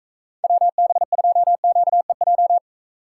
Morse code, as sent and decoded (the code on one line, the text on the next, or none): .-- -... ..--- --.- . .---
WB2QEJ